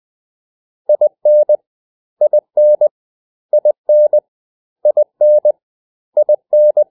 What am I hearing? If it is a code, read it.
Morse code, heard as a keyed tone, about 20 wpm, 600 Hz. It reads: IN IN IN IN IN
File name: cw / 100 Common English Words